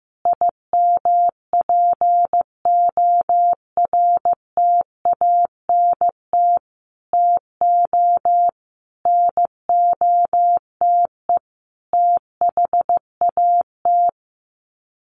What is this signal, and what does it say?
Morse code, heard as a keyed tone, about 15 wpm, 700 Hz. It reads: IMPORTANT TO NOTE THAT
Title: Sets of 5 Words 15wpm